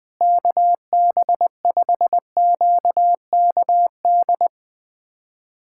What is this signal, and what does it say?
Morse code, heard as a keyed tone, about 20 wpm, 700 Hz. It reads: KB5QKD
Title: USA Callsign DE Callsign - 6 Characters - 20wpm